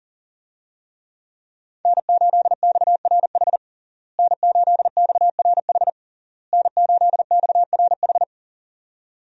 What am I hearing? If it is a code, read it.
Morse code, heard as a keyed tone, about 37 wpm, 700 Hz. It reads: N8XRH N8XRH N8XRH
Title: USA Call Signs - 5 Characters - Repeated 3x - 40wpm